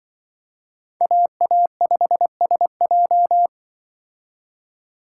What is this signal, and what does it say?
Morse code, heard as a keyed tone, about 24 wpm, 700 Hz. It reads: AA5SJ
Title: Speed Racing - USA Call Signs 20wpm